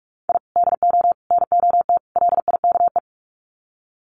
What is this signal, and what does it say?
Morse code, heard as a keyed tone, about 45 wpm, 700 Hz. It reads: I DO NOT LIKE
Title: Sets of 4 Words 45wpm